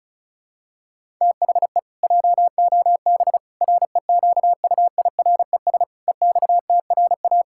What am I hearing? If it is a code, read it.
Morse code, heard as a keyed tone, about 36 wpm, 700 Hz. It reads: THE JOB REQUIRES EXTRA
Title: Pangram Challenge v2 - 35wpm